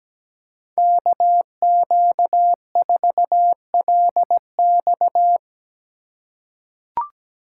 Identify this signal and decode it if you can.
Morse code, heard as a keyed tone, about 17 wpm, 700 Hz. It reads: KQ4LX
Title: USA Call Signs - 5 Characters - 3x Character Spacing - 17wpm